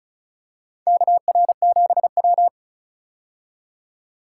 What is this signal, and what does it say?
Morse code, heard as a keyed tone, about 36 wpm, 700 Hz. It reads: KR7W
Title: K1USN SST QSO Practice - 35wpm